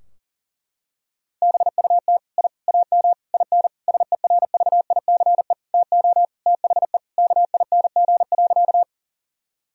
Morse code, heard as a keyed tone, about 40 wpm, 700 Hz. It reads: BUT I AM IN SERVICE TO THE KING.
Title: Sentences from Top 1000 Words - Review Part 4 - 40wpm